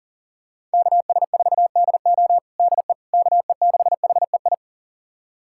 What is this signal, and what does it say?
Morse code, heard as a keyed tone, about 40 wpm, 700 Hz. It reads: KS4DO DE KE6HEI
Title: USA Callsign DE Callsign 40wpm